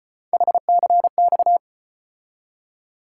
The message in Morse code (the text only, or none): HCX